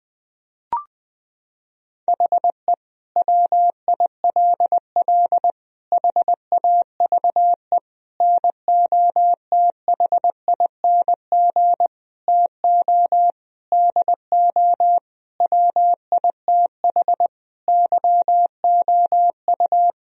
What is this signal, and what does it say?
Morse code, heard as a keyed tone, about 21 wpm, 700 Hz. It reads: HE WILL HAVE NOTHING TO DO WITH YOU
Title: Sentences from Top 1000 Words - Review Part 4 - 20wpm